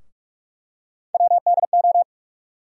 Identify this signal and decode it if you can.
Morse code, heard as a keyed tone, about 45 wpm, 700 Hz. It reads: WDO
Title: USA Call Sign Postfixes - 3 Letters 45wpm